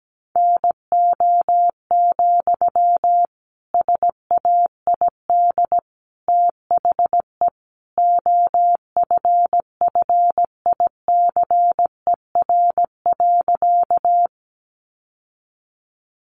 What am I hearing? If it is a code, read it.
Morse code, heard as a keyed tone, about 17 wpm, 700 Hz. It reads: NO, SAID THE OFFICER.
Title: Sentences from Top 1200 Words 17wpm